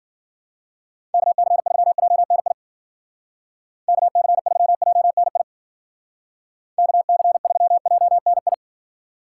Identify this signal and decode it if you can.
Morse code, heard as a keyed tone, about 52 wpm, 700 Hz. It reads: KC3JNI KC3JNI KC3JNI